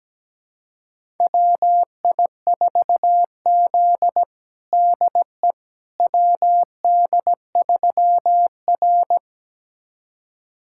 Morse code, heard as a keyed tone, about 17 wpm, 700 Hz. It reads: WI4Z DE WD3R
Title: USA Callsign DE Callsign - 4 Characters - 17wpm